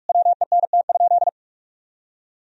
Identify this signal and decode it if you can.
Morse code, heard as a keyed tone, about 41 wpm, 700 Hz. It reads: WENT?
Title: Sentences from Top 1000 Words - Review Part 3 - 45wpm